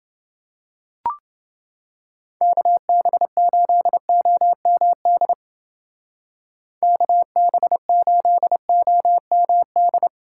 Morse code, heard as a keyed tone, about 29 wpm, 700 Hz. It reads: KB8OMD KB8OMD